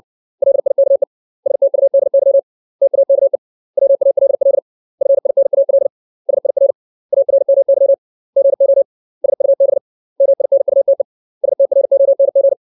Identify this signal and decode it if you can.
Morse code, heard as a keyed tone, about 62 wpm, 550 Hz. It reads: LIKE STUDY TAKE WALL FINAL SIR ARMY GO SUB MINUTE STRONG
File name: 632 words / 632 Words 60WPM